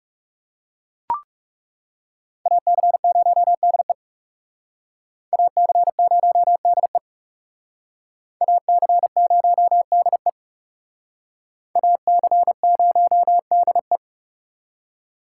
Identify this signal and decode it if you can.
Morse code, heard as a keyed tone, about 37 wpm, 700 Hz. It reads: AC0DE AC0DE AC0DE AC0DE
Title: Speed Racing - USA Call Signs 30wpm